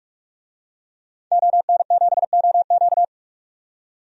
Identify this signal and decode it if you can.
Morse code, heard as a keyed tone, about 43 wpm, 700 Hz. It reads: ON7OQ
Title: DXCC-100 Call Signs - 5 Characters - 45wpm